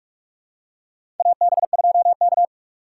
Morse code, heard as a keyed tone, about 46 wpm, 700 Hz. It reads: AB2K
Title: USA Call Signs - 4 Characters - 6x Character Spacing - 45wpm